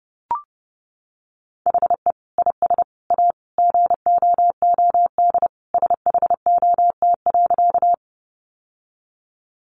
Morse code, heard as a keyed tone, about 30 wpm, 700 Hz. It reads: HE IS A GOOD SHOT.
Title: Sentences from Top 1100 Words 30wpm